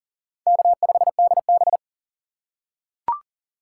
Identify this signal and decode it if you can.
Morse code, heard as a keyed tone, about 40 wpm, 700 Hz. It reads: K5DB
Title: USA Call Signs - 6x Character Spacing - 40wpm